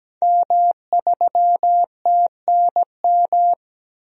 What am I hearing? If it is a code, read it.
Morse code, heard as a keyed tone, about 17 wpm, 700 Hz. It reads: M3TNM
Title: DXCC-100 Call Signs - 5 Characters - Repeated 2x - 17wpm